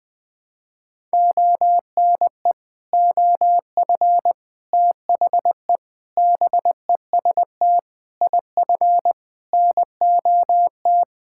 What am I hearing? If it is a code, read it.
Morse code, heard as a keyed tone, about 21 wpm, 700 Hz. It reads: ONE OF THE BEST IF NOT
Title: Sets of 8 Words - 20wpm